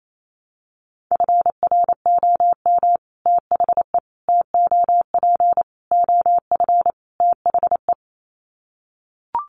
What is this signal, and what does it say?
Morse code, heard as a keyed tone, about 28 wpm, 700 Hz. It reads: FROM THE TOP OF THE
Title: Sets of 5 Words Encore 28wpm